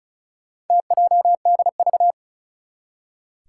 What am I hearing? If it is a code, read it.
Morse code, heard as a keyed tone, about 36 wpm, 700 Hz. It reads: TJDV
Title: Four Challenging Letter-Number Combinations 35wpm